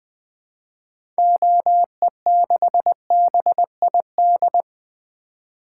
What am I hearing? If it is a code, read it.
Morse code, heard as a keyed tone, about 21 wpm, 700 Hz. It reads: OE6BID
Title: 2023 SOTA Activator Call Signs - 20wpm